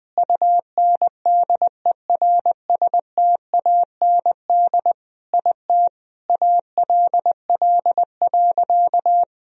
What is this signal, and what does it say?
Morse code, heard as a keyed tone, about 20 wpm, 700 Hz. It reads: UNDERSTAND IT ALL.
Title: Sentences from Top 1000 Words - Review Part 4 - 20wpm